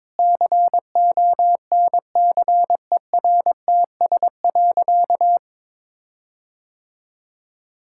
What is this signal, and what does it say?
Morse code, heard as a keyed tone, about 22 wpm, 700 Hz. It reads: CONCERTS.
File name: practice / 22wpm / Sentences with 8-Word Phrases - 22wpm